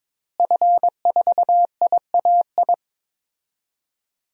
Morse code, heard as a keyed tone, about 22 wpm, 700 Hz. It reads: F4IAI
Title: DXCC-100 Call Signs - 5 Characters - Repeated 3x - 22wpm